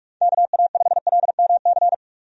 Morse code, heard as a keyed tone, about 42 wpm, 700 Hz. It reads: KA5LMC
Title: USA Call Signs - 6 Characters - 45wpm